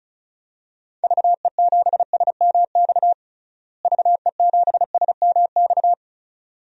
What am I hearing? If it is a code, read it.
Morse code, heard as a keyed tone, about 34 wpm, 700 Hz. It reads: VE7SMX VE7SMX
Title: DXCC-100 Call Signs - 6 Characters - Repeated 2x - 35wpm